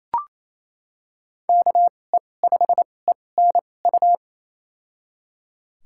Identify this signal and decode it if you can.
Morse code, heard as a keyed tone, about 28 wpm, spaced as for 20 wpm, 700 Hz. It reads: KE5ENU
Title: USA Call Signs - 2x Character Spacing - 28wpm